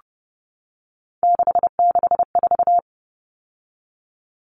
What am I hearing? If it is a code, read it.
Morse code, heard as a keyed tone, about 30 wpm, 700 Hz. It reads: 664
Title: Common Cut Numbers - 3-Digit 30wpm